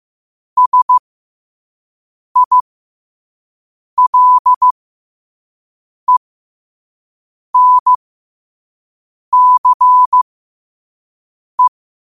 Morse code, heard as a keyed tone, about 15 wpm, spaced as for 5 wpm, 1000 Hz. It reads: SILENCE
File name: qotdfiles / 5wpm